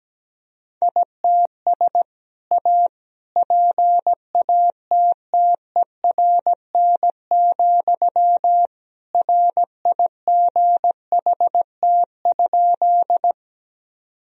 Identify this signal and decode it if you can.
Morse code, heard as a keyed tone, about 17 wpm, 700 Hz. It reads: ITS A PATTERN, RIGHT?
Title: Sentences from Top 1000 Words - Review Part 2 - 17wpm